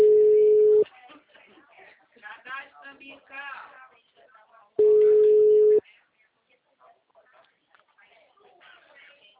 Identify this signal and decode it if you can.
Ringback tone (Europe, 425 Hz), with speech.
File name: out- -2010- - - .62462